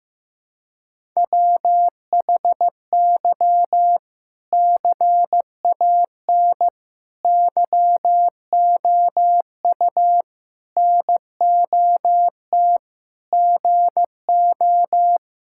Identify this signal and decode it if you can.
Morse code, heard as a keyed tone, about 15 wpm, 700 Hz. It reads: WHY CAN YOU NOT GO
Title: Sentences from Top 1600 Words 15wpm